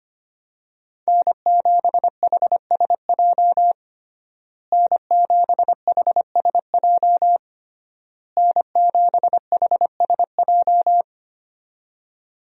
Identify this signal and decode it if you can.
Morse code, heard as a keyed tone, about 25 wpm, 700 Hz. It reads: N7HSJ N7HSJ N7HSJ
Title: USA Call Signs - 5 Characters - Repeated 3x - 25wpm